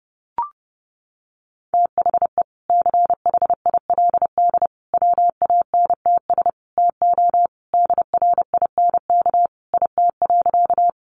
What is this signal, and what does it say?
Morse code, heard as a keyed tone, about 30 wpm, 700 Hz. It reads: THE CHILD WANTS TO DRINK IT.
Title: Additional Easy Phrases 30wpm